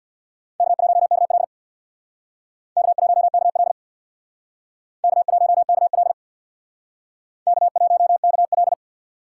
Morse code, heard as a keyed tone, about 62 wpm, 700 Hz. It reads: K1KL K1KL K1KL K1KL
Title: Speed Racing - USA 1x2 Call Signs 50wpm